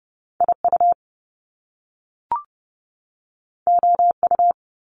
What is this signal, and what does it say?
Morse code, heard as a keyed tone, about 30 wpm, 700 Hz. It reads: IU OU
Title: Top Two Letter Combinations in Words 30wpm